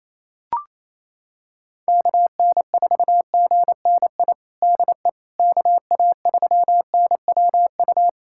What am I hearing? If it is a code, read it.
Morse code, heard as a keyed tone, about 28 wpm, 700 Hz. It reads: KN4GNI DE KA3NWU
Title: USA Callsign DE Callsign - 6 Characters - 28wpm